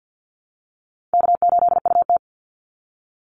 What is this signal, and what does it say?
Morse code, heard as a keyed tone, about 50 wpm, 700 Hz. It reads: K8UT
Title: Easy Call Signs Repeated 3x - 50wpm